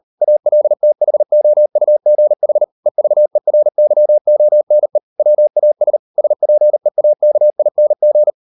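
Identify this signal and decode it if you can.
Morse code, heard as a keyed tone, about 38 wpm, 600 Hz. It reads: ALTHOUGH EVERYONE WAS SPEAKING
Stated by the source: Morse code practice at 38 WPM, sending Ken Follett's novel Never in English, from CW med Gnister.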